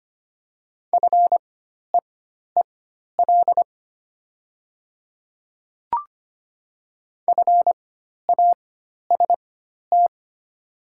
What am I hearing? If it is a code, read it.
Morse code, heard as a keyed tone, about 25 wpm, spaced as for 12 wpm, 700 Hz. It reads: FEEL FAST
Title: 4x Character Spacing - Top 500 Words - 25wpm